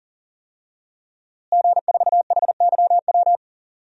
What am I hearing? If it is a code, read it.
Morse code, heard as a keyed tone, about 37 wpm, 700 Hz. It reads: G4HYW
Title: DXCC-100 Call Signs - 5 Characters - Repeated 3x - 40wpm